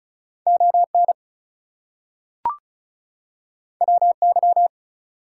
Morse code, heard as a keyed tone, about 36 wpm, 700 Hz. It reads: ON WY
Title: US States and Canadian Provinces 35wpm